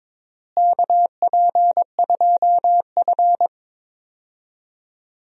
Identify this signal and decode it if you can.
Morse code, heard as a keyed tone, about 22 wpm, 700 Hz. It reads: KP2F
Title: DXCC-100 Call Signs - 4 Characters - Repeated 3x - 22wpm